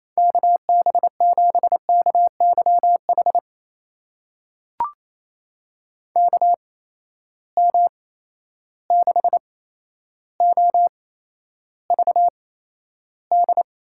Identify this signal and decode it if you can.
Morse code, heard as a keyed tone, about 27 wpm, spaced as for 12 wpm, 700 Hz. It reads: KB7KYH KM6OVD
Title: USA Call Signs - 6 Characters - 8x Character Spacing - 28wpm